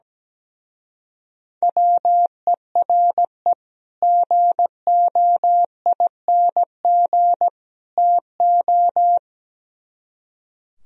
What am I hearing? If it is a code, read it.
Morse code, heard as a keyed tone, about 19 wpm, 700 Hz. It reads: WERE GOING TO
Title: Speed-Racing - Sets of 3 Words 17wpm